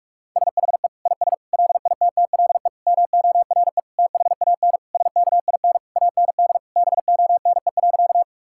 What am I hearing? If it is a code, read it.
Morse code, heard as a keyed tone, about 46 wpm, 700 Hz. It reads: SHE IS LITTLE MORE THAN SKIN AND BONE.
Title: Sentences from Top 1600 Words 45wpm